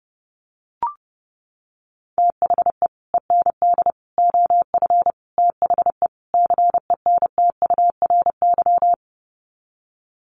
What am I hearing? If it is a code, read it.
Morse code, heard as a keyed tone, about 28 wpm, 700 Hz. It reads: THE END OF THE CENTURY
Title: Sets of 5 Words Encore2 30wpm